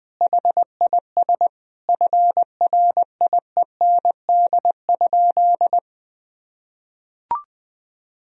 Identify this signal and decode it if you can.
Morse code, heard as a keyed tone, about 21 wpm, 700 Hz. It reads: HIS FRIEND?
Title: Sentences from Top 1000 Words - Review Part 3 - 20wpm